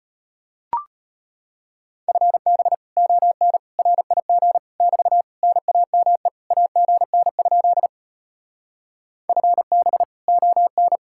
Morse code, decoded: FB ON RIG <BT> NAME AGN? FB ON

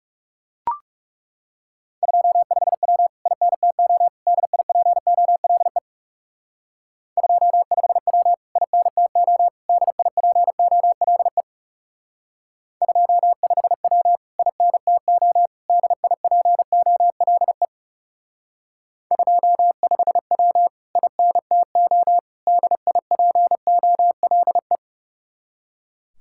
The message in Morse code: 25W INTO DIPOLE 25W INTO DIPOLE 25W INTO DIPOLE 25W INTO DIPOLE

